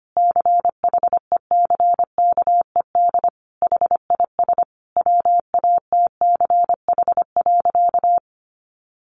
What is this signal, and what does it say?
Morse code, heard as a keyed tone, about 25 wpm, 700 Hz. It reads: CHECKED HIS WATCH.
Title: A Sound Of Thunder 25wpm